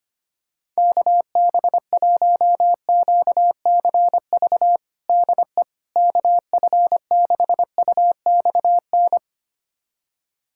Morse code, heard as a keyed tone, about 25 wpm, 700 Hz. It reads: KB1QCV DE KF6UXN
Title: USA Callsign DE Callsign - 6 Characters - 25wpm